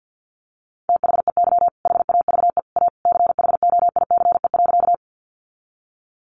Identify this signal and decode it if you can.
Morse code, heard as a keyed tone, about 50 wpm, 700 Hz. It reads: THEY HAVE A CHOICE.
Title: Sentences from Top 900 Words 50wpm